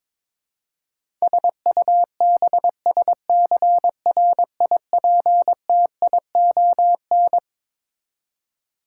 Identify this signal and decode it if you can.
Morse code, heard as a keyed tone, about 21 wpm, 700 Hz. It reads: SUBSCRIPTION
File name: Twelve Letter Words - 22wpm